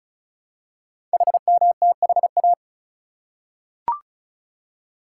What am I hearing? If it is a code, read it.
Morse code, heard as a keyed tone, about 36 wpm, 700 Hz. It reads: HMTHA
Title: Code Groups - 5 Letters - 35wpm